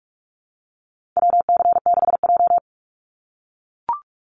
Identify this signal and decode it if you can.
Morse code, heard as a keyed tone, about 44 wpm, 700 Hz. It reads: WC6J
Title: NA SOTA Activator Callsigns 45wpm